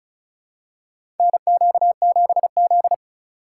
Morse code, heard as a keyed tone, about 36 wpm, 700 Hz. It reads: NQ7Z
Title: USA Call Signs - 4 Characters - 2x Character Spacing - 35wpm